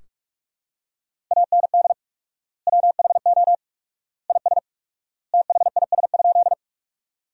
Morse code, heard as a keyed tone, about 46 wpm, 700 Hz. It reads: AND WHO IS THIS?
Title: 4x Word-Spacing - Sentences from Top 100 Words - 45wpm